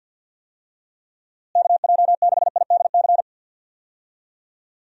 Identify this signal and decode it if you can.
Morse code, heard as a keyed tone, about 46 wpm, 700 Hz. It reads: KJ6IDC
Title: USA Call Signs - 6 Characters - Repeated 2x - 50wpm